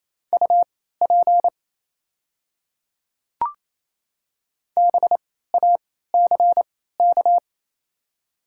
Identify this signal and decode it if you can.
Morse code, heard as a keyed tone, about 29 wpm, spaced as for 16 wpm, 700 Hz. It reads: UP BACK
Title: 3x Character Spacing - Top 200 Words - 28wpm